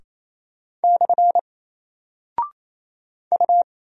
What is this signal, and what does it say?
Morse code, heard as a keyed tone, about 27 wpm, 700 Hz. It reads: / U
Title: Single Character (Letter:Number:Punctuation:Prosign) Alt 28wpm